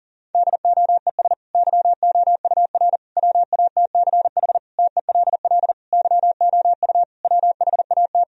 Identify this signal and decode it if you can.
Morse code, heard as a keyed tone, about 40 wpm, 700 Hz. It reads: DOES YOUR WATCH TELL YOU WHAT
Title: Sentences from Top 1000 Words - Review Part 1 - 40wpm